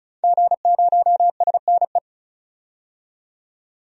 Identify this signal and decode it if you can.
Morse code, heard as a keyed tone, about 36 wpm, 700 Hz. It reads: G0SNE
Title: DXCC-100 Call Signs - 5 Characters - 35wpm